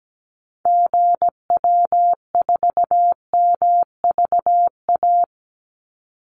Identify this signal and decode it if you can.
Morse code, heard as a keyed tone, about 17 wpm, 700 Hz. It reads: GW4MVA
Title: Call Signs - 17wpm